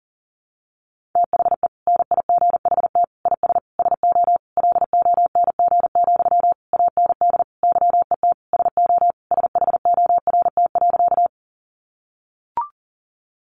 Morse code, heard as a keyed tone, about 40 wpm, 700 Hz. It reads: THE NIGHT IS SO LONG, AND YET SO SHORT.
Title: Sentences from Top 500 Words 40wpm